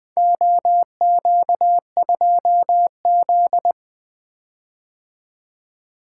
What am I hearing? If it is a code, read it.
Morse code, heard as a keyed tone, about 20 wpm, 700 Hz. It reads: OQ2Z
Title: Four Letter-Number Combinations - 20wpm